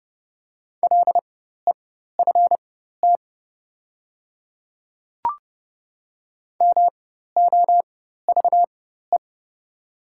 Morse code, heard as a keyed tone, about 31 wpm, spaced as for 14 wpm, 690 Hz. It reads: LEFT MOVE
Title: 4x Character Spacing - Top 500 Words - 30wpm